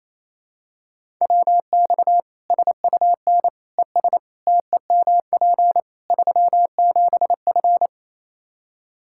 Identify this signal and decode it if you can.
Morse code, heard as a keyed tone, about 28 wpm, 700 Hz. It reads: WX SUN ES TEMP 37F
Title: Practice QSO - Exchange 2 - Info - 28wpm